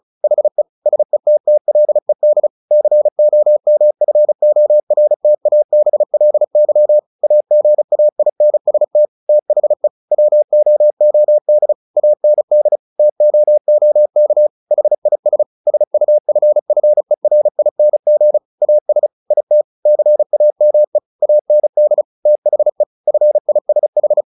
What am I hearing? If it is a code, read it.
Morse code, heard as a keyed tone, about 34 wpm, 600 Hz. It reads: HE SETTLED COMFORTABLY AGAINST THE WOOD AND TOOK HIS SUFFERING AS IT CAME AND THE FISH
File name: the old man and the sea / the old man and the sea 1 35wpm